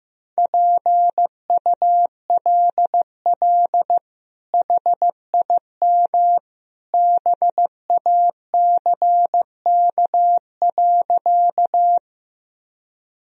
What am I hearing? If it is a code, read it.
Morse code, heard as a keyed tone, about 15 wpm, 700 Hz. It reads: PULL HIM BACK.
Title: Sentences from Top 1000 Words - Review Part 2 - 15wpm